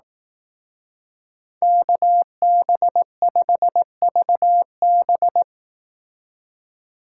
Morse code, heard as a keyed tone, about 18 wpm, 700 Hz. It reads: KB5VB
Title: Speed Racing - USA 2x2 Call Signs 15wpm